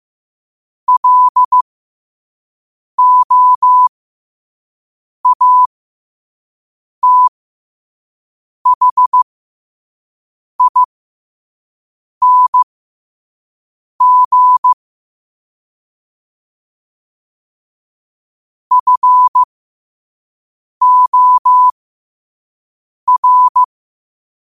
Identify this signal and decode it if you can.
Morse code, heard as a keyed tone, about 15 wpm, spaced as for 5 wpm, 1000 Hz. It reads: LOATHING FOR
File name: qotdfiles / 5wpm